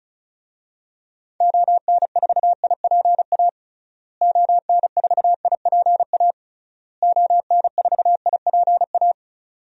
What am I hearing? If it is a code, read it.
Morse code, heard as a keyed tone, about 36 wpm, 700 Hz. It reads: ON4IPA ON4IPA ON4IPA
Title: DXCC-100 Call Signs - 6 Characters - Repeated 3x - 35wpm